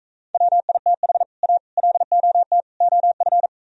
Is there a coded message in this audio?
Yes: Morse code from 0.3 to 3.5 s, heard as a keyed tone, about 45 wpm, 700 Hz. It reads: WITH A LOT OF